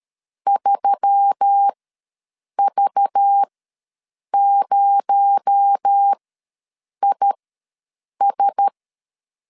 Morse code, heard as a keyed tone, about 13 wpm, spaced as for 7 wpm, 800 Hz. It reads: 3V0IS
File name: LEZIONI / 6 PKX0 / 6 PKX0 ws 12wpm 3448